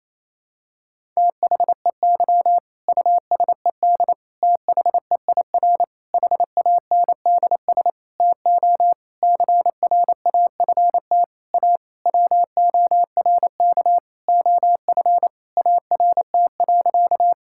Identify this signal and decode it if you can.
Morse code, heard as a keyed tone, about 28 wpm, 690 Hz. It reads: THEY USED THEIR HANDS TO CRAFT A WORK OF ART.
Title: Sentences from Top 1000 Words - Review Part 6 - 28wpm